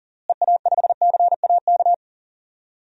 Morse code, heard as a keyed tone, about 40 wpm, 700 Hz. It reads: EA5CAK
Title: DXCC-100 Call Signs - 6 Characters - Repeated 3x - 40wpm